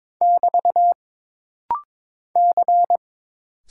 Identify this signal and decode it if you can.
Morse code, heard as a keyed tone, about 22 wpm, 700 Hz. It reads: <BT> C
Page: This rapid-fire timing is geared to push your ICR (Instant Character Recognition) skills.